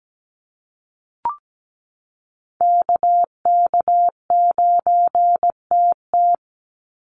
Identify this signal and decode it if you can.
Morse code, heard as a keyed tone, about 17 wpm, 700 Hz. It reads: KK9TT
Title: USA Call Signs 17wpm